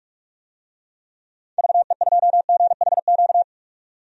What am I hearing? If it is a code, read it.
Morse code, heard as a keyed tone, about 46 wpm, 700 Hz. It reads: VE2GHQ
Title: DXCC-100 Call Signs - 6 Characters - 45wpm